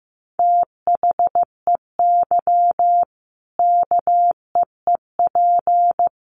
Morse code, heard as a keyed tone, about 15 wpm, 700 Hz. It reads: THEY KEEP
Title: Sentences from Top 1500 Words 15wpm